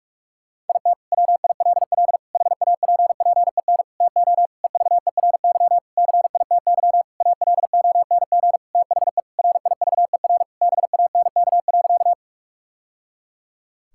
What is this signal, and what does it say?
Morse code, heard as a keyed tone, about 41 wpm, 700 Hz. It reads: IT WILL HAPPEN TO EVERY CITY ALONG THE RIVER BANK.
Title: Sentences from Top 1000 Words - Review Part 6 - 45wpm